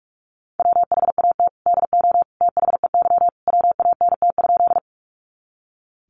Morse code, heard as a keyed tone, about 44 wpm, 700 Hz. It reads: WHAT DO THEY WANT?
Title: Sentences from Top 200 Words 45wpm